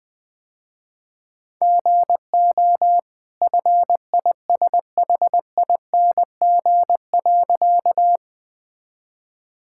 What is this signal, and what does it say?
Morse code, heard as a keyed tone, about 20 wpm, 700 Hz. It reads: GO FISHING.
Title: Easy Sentences with Hints - 20wpm